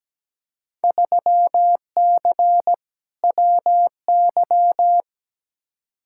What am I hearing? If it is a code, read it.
Morse code, heard as a keyed tone, about 17 wpm, 700 Hz. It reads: 3C WY
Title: Field Day Exchanges Encore - 17wpm